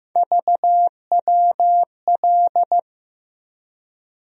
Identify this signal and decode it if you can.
Morse code, heard as a keyed tone, about 16 wpm, 700 Hz. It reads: VWL
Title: USA Call Sign Postfixes - 3 Letters 15wpm